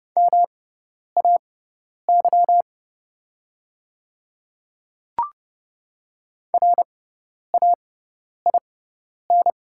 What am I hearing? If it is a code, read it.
Morse code, heard as a keyed tone, about 30 wpm, spaced as for 10 wpm, 690 Hz. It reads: MAY RAIN
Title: 6x Character Spacing - Top 300 Words - 30wpm